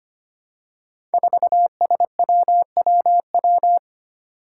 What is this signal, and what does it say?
Morse code, heard as a keyed tone, about 25 wpm, 700 Hz. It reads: 4SWWW
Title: Five Challenging Letter-Number Combinations 25wpm